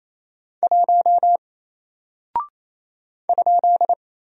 Morse code, heard as a keyed tone, about 27 wpm, 700 Hz. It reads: 1 ?
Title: Single Character (Letter:Number:Punctuation:Prosign) 28wpm